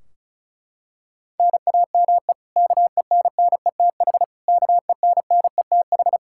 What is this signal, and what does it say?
Morse code, heard as a keyed tone, about 34 wpm, 700 Hz. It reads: NAME KENNETH KENNETH
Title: Practice QSO - Exchange 1 - Beginning - 35wpm